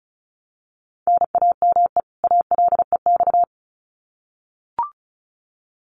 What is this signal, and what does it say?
Morse code, heard as a keyed tone, about 34 wpm, 700 Hz. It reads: NAME ALEX
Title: Common QSO Phrases 35wpm